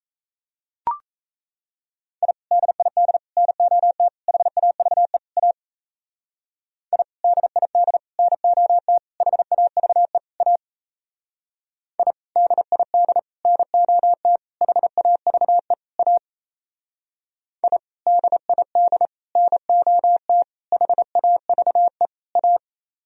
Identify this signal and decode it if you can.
Morse code, heard as a keyed tone, about 36 wpm, 700 Hz. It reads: I DID NOT HAVE A I DID NOT HAVE A I DID NOT HAVE A I DID NOT HAVE A